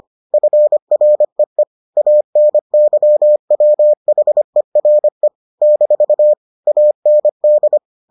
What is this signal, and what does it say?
Morse code, heard as a keyed tone, about 25 wpm, 600 Hz. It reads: FREE ANYWHERE - AND